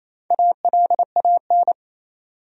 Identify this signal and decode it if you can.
Morse code, heard as a keyed tone, about 27 wpm, 700 Hz. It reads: ALAN
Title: USA Common Names - Max 4 Letters - 28wpm